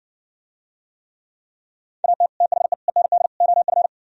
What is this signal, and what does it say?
Morse code, heard as a keyed tone, about 61 wpm, 700 Hz. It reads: AT THE END OF